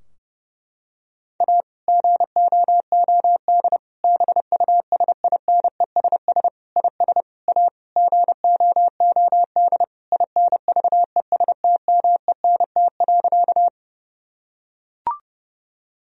Morse code, heard as a keyed tone, about 30 wpm, 700 Hz. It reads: A GOOD BUSINESS IS A GOOD INVESTMENT.
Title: Sentences from Top 1000 Words - Review Part 4 - 30wpm